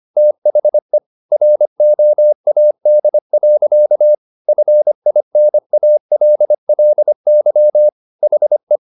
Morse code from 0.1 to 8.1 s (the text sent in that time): THE ROAD. FINALLY